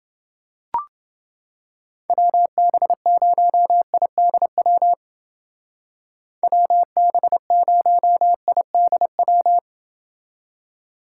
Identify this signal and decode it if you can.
Morse code, heard as a keyed tone, about 29 wpm, 700 Hz. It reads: WB0IDW WB0IDW